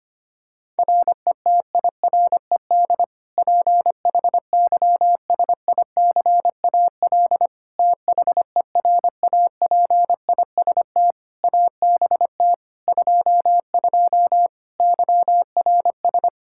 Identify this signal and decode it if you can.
Morse code, heard as a keyed tone, about 25 wpm, 700 Hz. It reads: RETIRED PHYSICAL THERAPIST ABT 22 YRS
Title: Practice QSO - Exchange 3 - More Info - 25wpm